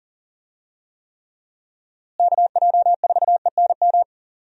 Morse code, heard as a keyed tone, about 40 wpm, 700 Hz. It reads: KJ4ENM
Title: USA Call Signs - 6 Characters - Repeated 2x - 40wpm